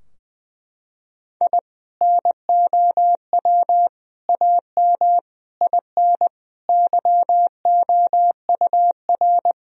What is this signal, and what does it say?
Morse code, heard as a keyed tone, about 20 wpm, 700 Hz. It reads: I NOW AM IN YOUR
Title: Sentences from Top 1000 Words - Review Part 2 - 20wpm